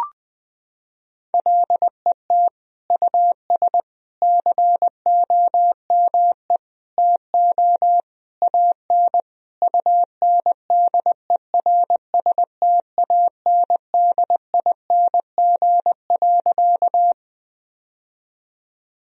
Morse code, heard as a keyed tone, about 21 wpm, 700 Hz. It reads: LET US COME TO AN UNDERSTANDING.
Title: Sentences from Top 1000 Words - Review Part 3 - 20wpm